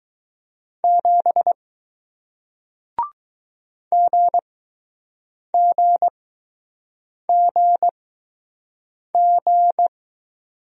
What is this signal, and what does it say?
Morse code, heard as a keyed tone, about 20 wpm, spaced as for 6 wpm, 700 Hz. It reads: 7 GGGG